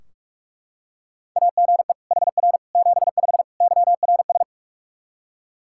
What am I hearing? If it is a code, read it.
Morse code, heard as a keyed tone, about 43 wpm, 700 Hz. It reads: AGE HR 75 YRS
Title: K1USN SST QSO Practice 45wpm